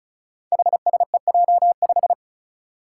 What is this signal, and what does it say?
Morse code, heard as a keyed tone, about 34 wpm, 700 Hz. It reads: HSEJ5
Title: Five Challenging Letter-Number Combinations 35wpm